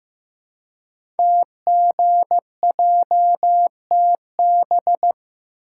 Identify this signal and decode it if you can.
Morse code, heard as a keyed tone, about 15 wpm, 700 Hz. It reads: TGJTB
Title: Code Groups - 5 Letters - 15wpm